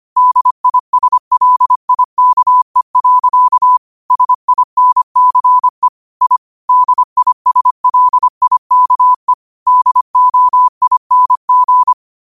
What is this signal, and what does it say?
Morse code, heard as a keyed tone, about 25 wpm, 1000 Hz. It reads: DISLIKE. SINCE I DISLIKE DOING